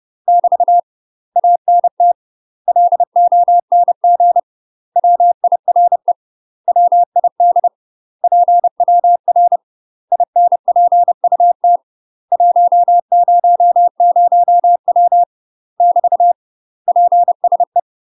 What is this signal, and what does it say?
Morse code, heard as a keyed tone, about 30 wpm, 700 Hz. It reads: <BT> ANT LONG WIRE WID PWR INPUT 100W <BT> PSE
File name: UK MASTER QSOs TRAINING / 30WPM / UK MASTER QSOs 0038